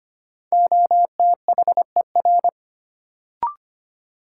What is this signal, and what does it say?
Morse code, heard as a keyed tone, about 25 wpm, 700 Hz. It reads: OTHER
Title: Sentences from Top 100 Words - New Word Drill - 25wpm